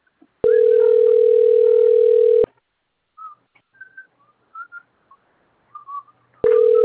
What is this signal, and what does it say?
Ringback tone (North America), 440 + 480 Hz.